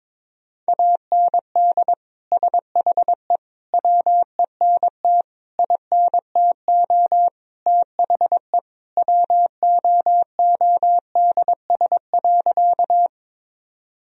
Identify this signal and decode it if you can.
Morse code, heard as a keyed tone, about 22 wpm, 700 Hz. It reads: AND SHE WENT INTO THE WOODS.
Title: Sentences from Top 1000 Words - Review Part 2 - 22wpm